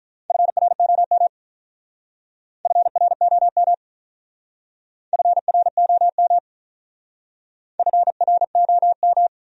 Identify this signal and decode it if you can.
Morse code, heard as a keyed tone, about 44 wpm, 700 Hz. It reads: FROM FROM FROM FROM